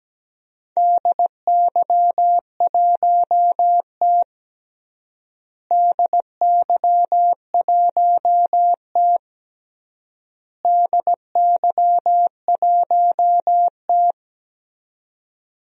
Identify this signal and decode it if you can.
Morse code, heard as a keyed tone, about 17 wpm, 700 Hz. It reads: DY1T DY1T DY1T
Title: DXCC-100 Call Signs - 4 Characters - Repeated 3x - 17wpm